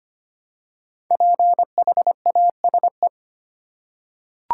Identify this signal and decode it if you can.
Morse code, heard as a keyed tone, about 25 wpm, 700 Hz. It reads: PHASE
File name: 5 Letter Words - First 3 Letters Given - 25wpm